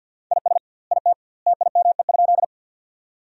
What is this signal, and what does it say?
Morse code, heard as a keyed tone, about 50 wpm, 700 Hz. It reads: IS IT TIME?
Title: 2x Word-Spacing - Sentences from Top 100 Words - 50wpm